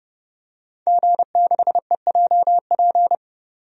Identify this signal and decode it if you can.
Morse code, heard as a keyed tone, about 32 wpm, 700 Hz. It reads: G6EJP
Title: DXCC-100 Call Signs - 5 Characters - 30wpm